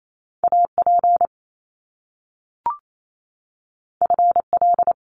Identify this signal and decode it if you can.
Morse code, heard as a keyed tone, about 28 wpm, 700 Hz. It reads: AP FL
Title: Top Two Letter Combinations in Words 28wpm